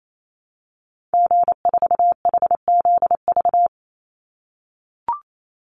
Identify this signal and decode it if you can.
Morse code, heard as a keyed tone, about 28 wpm, 700 Hz. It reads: G4HZV
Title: Call Signs - 28wpm